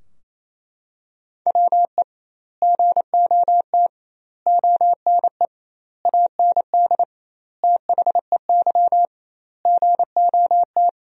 Morse code, heard as a keyed tone, about 27 wpm, 700 Hz. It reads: WE GOT ONE AND THEY GOT
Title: 2x Word-Spacing - Sentences from Top 300 Words - 28wpm